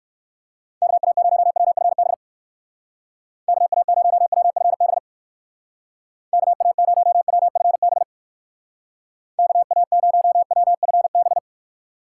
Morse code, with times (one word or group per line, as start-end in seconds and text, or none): 0.8-2.1 KA0WFB
3.5-5.0 KA0WFB
6.3-8.0 KA0WFB
9.4-11.4 KA0WFB